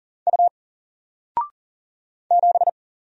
Morse code, heard as a keyed tone, about 39 wpm, 700 Hz. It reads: U 7
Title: Single Character (Letter:Number:Punctuation:Prosign) 40wpm